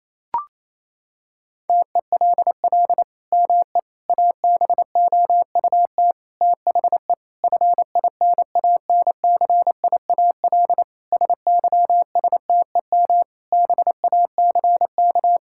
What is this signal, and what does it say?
Morse code, heard as a keyed tone, about 27 wpm, 700 Hz. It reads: TELL ME ABOUT THE FINANCIAL SYSTEM BACK
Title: Sentences from Top 1000 Words - Review Part 5 - 28wpm